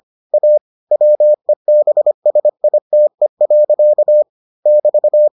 Morse code, heard as a keyed tone, about 25 wpm, 600 Hz. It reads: A WEBSITE. <BT>